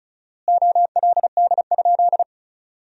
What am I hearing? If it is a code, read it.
Morse code, heard as a keyed tone, about 36 wpm, 700 Hz. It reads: OLD?
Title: 4x Word-Spacing - Sentences from Top 400 Words - 35wpm